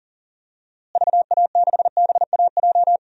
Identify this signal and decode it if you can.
Morse code, heard as a keyed tone, about 40 wpm, 700 Hz. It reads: VA6BAJ
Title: DXCC-100 Call Signs - 6 Characters - Repeated 3x - 40wpm